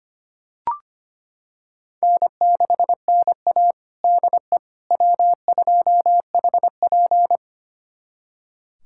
Morse code, heard as a keyed tone, about 25 wpm, 700 Hz. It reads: N6NA DE W2HP
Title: USA Callsign DE Callsign - 4 Characters - 25wpm